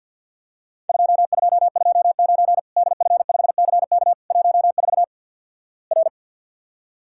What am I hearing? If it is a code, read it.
Morse code, heard as a keyed tone, about 50 wpm, 700 Hz. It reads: 2229 DF5CK 14 R
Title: Worked All Europe DX Contest - QTC Exchange - 50wpm